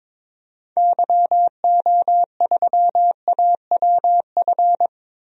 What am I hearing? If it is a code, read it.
Morse code, heard as a keyed tone, about 22 wpm, 700 Hz. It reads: YO3AWF
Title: DXCC-100 Call Signs - 6 Characters - 22wpm